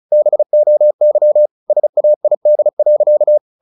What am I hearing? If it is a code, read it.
Morse code, heard as a keyed tone, about 34 wpm, 600 Hz. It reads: BOY SAID.
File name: the old man and the sea / the old man and the sea 1 35wpm